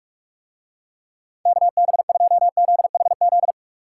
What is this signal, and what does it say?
Morse code, heard as a keyed tone, about 44 wpm, 700 Hz. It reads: KB2ZHZ